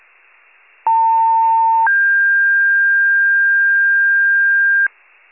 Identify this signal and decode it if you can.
Two-tone page, heard as a one-second tone, then a three-second tone, about 900 then 1600 Hz, with noise. It is heard from 0.9 to 4.9 s.
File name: wcalert